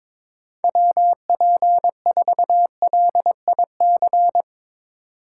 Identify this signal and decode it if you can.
Morse code, heard as a keyed tone, about 22 wpm, 700 Hz. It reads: WP4LIC
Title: Speed Racing - USA 2x3 Call Signs 22wpm